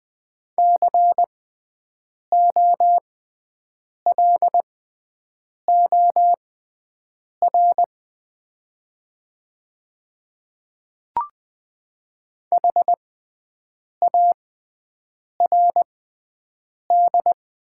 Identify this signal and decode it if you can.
Morse code, heard as a keyed tone, about 20 wpm, spaced as for 7 wpm, 700 Hz. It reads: COLOR HARD
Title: 6x Character Spacing - Top 300 Words - 20wpm